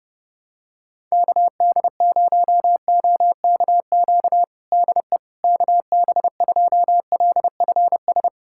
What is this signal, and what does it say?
Morse code, heard as a keyed tone, about 30 wpm, 700 Hz. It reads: KD0OKQ DE KB2LFS
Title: USA Callsign DE Callsign - 6 Characters - 30wpm